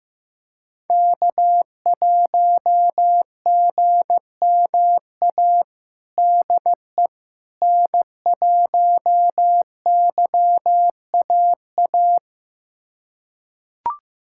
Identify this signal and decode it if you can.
Morse code, heard as a keyed tone, about 16 wpm, 700 Hz. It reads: K1GMA DE N1YAA
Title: USA Callsign DE Callsign - 5 Characters - 15wpm